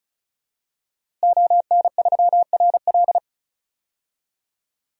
Morse code, heard as a keyed tone, about 36 wpm, 700 Hz. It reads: ON3RL
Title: DXCC-100 Call Signs - 5 Characters - Repeated 3x - 35wpm